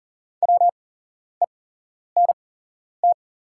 Morse code, heard as a keyed tone, about 40 wpm, spaced as for 11 wpm, 690 Hz. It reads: WENT
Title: 8x Character Spacing - Top 200 Words - 40wpm